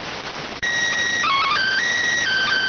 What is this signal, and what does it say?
SSTV (slow-scan television), PD 290.